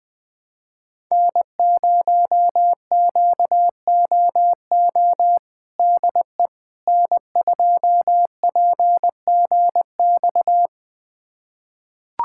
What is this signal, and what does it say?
Morse code, heard as a keyed tone, about 21 wpm, 700 Hz. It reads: N0QOO DE N2PGX
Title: USA Callsign DE Callsign - 5 Characters - 20wpm